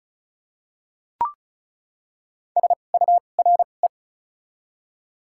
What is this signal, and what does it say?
Morse code, heard as a keyed tone, about 34 wpm, spaced as for 25 wpm, 700 Hz. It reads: SURE
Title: 2x Character Spacing - Top 300 Words - 35wpm